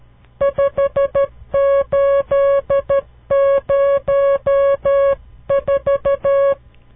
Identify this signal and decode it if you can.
Morse code, heard as a keyed tone, about 12 wpm, 560 Hz. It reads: 5804
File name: CW Learning Programs / farnsworth / unit 5